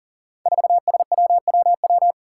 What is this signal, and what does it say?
Morse code, heard as a keyed tone, about 41 wpm, 700 Hz. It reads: 4SWWW
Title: Five Challenging Letter-Number Combinations 40wpm